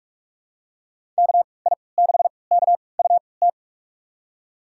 Morse code, heard as a keyed tone, about 46 wpm, spaced as for 26 wpm, 700 Hz. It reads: KI6KUT
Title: USA Call Signs - 3x Character Spacing - 45wpm